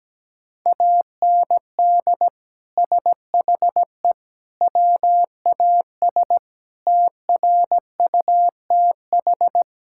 Morse code, heard as a keyed tone, about 17 wpm, 700 Hz. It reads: AND SHE WAS TRUTH
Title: Sentences from Top 1000 Words - Review Part 5 - 17wpm